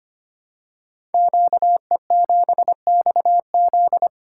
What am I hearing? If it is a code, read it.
Morse code, heard as a keyed tone, about 25 wpm, 700 Hz. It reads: QE7XZ
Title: Code Groups - 5 Letter:Numbers - 25wpm